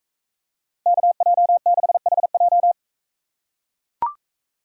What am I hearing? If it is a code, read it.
Morse code, heard as a keyed tone, about 43 wpm, 700 Hz. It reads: KJ6HJ